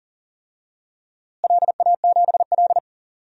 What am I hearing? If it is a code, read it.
Morse code, heard as a keyed tone, about 40 wpm, 700 Hz. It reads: LA7L
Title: DXCC-100 Call Signs - 4 Characters - Repeated 2x - 40wpm